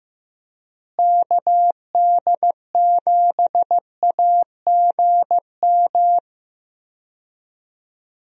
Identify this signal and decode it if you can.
Morse code, heard as a keyed tone, about 15 wpm, 700 Hz. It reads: KD7AGM
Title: USA Call Signs - 6 Characters - Repeated 2x - 15wpm